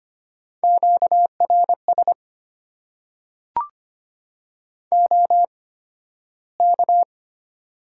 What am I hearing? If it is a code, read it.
Morse code, heard as a keyed tone, about 25 wpm, spaced as for 13 wpm, 700 Hz. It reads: QRS OK
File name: Top 90 QSO Elements - 8x Character Spacing - 25wpm